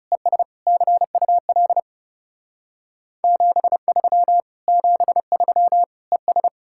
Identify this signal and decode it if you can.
Morse code, heard as a keyed tone, about 36 wpm, 700 Hz. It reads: ES CUL 73 73 ES
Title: Speed-Racing Common QSO Phrases - 30wpm